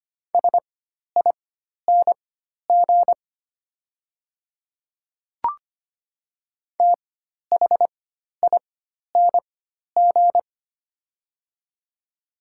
Morse code, heard as a keyed tone, about 25 wpm, spaced as for 12 wpm, 700 Hz. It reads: SING THING
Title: 4x Character Spacing - Top 500 Words - 25wpm